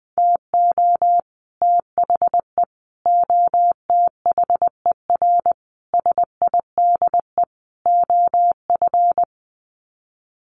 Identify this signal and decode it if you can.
Morse code, heard as a keyed tone, about 20 wpm, 700 Hz. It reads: TO THE OTHER SIDE OF
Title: Sets of 5 Words Encore 20wpm